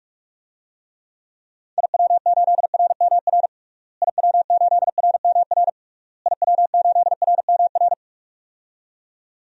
Morse code, heard as a keyed tone, about 44 wpm, 700 Hz. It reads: IW8RMR IW8RMR IW8RMR
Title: DXCC-100 Call Signs - 6 Characters - Repeated 3x - 45wpm